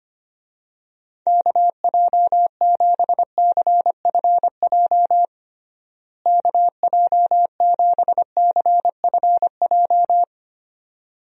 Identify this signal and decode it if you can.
Morse code, heard as a keyed tone, about 25 wpm, 700 Hz. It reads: KJ7CFJ KJ7CFJ
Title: USA Call Signs - 6 Characters - Repeated 2x - 25wpm